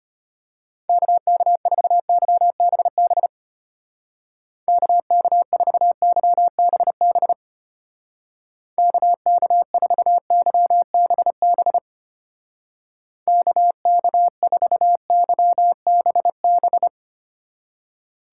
Morse code, heard as a keyed tone, about 31 wpm, 700 Hz. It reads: KK4YBB KK4YBB KK4YBB KK4YBB
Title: Speed Racing - USA 2x3 Call Signs 25wpm